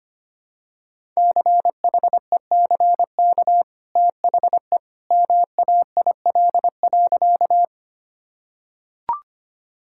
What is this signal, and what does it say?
Morse code, heard as a keyed tone, about 25 wpm, 700 Hz. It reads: CHECK THE MAIL.
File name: Easy Sentences with Hints - 25wpm